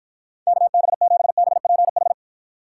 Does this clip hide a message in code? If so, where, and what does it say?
Yes: Morse code from 0.5 to 2.1 s, heard as a keyed tone, about 52 wpm, 700 Hz. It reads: KB7BPH